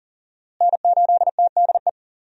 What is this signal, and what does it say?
Morse code, heard as a keyed tone, about 40 wpm, 700 Hz. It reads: N8TDE
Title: USA Call Signs - 5 Characters - Repeated 3x - 40wpm